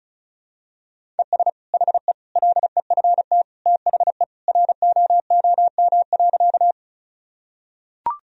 Morse code, heard as a keyed tone, about 36 wpm, 700 Hz. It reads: ES HE LEFT THE ROOM.
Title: Sentences from Top 300 Words and Abbreviations 35wpm